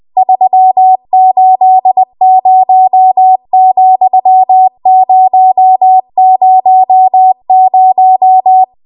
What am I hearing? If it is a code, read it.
Morse code, heard as a keyed tone, about 20 wpm, 750 Hz. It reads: 380,000
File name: Archive / 20 WPM / 20WPM